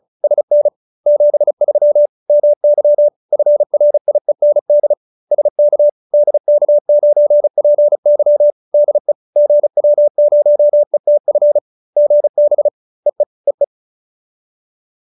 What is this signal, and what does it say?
Morse code, heard as a keyed tone, about 37 wpm, 590 Hz. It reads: SN 73 MY FRIEND SK DK9PY DE GW0ETF GB EE EE